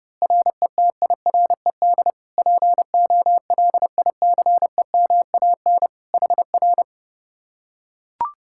Morse code, heard as a keyed tone, about 32 wpm, 700 Hz. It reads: RETIRED POLICEMAN HR